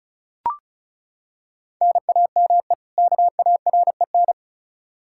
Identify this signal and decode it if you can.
Morse code, heard as a keyed tone, about 34 wpm, 700 Hz. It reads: NAME KAREN
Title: Practice QSO - Exchange 1 - Beginning - 35wpm